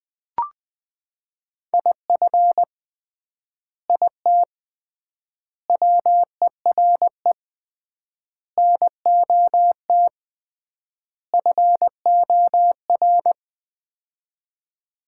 Morse code, heard as a keyed tone, about 21 wpm, 700 Hz. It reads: IF IT WERE NOT FOR
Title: 3x Word-Spacing - Sets of 5 Words - 20wpm